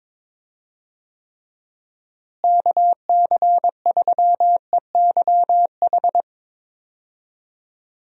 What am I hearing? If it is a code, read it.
Morse code, heard as a keyed tone, about 22 wpm, 700 Hz. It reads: KC3EYH
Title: USA Call Signs - 6 Characters - Repeated 2x - 22wpm